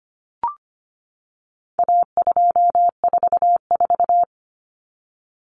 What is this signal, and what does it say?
Morse code, heard as a keyed tone, about 25 wpm, 700 Hz. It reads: A244
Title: Common Cut Numbers - 4-Digit 25wpm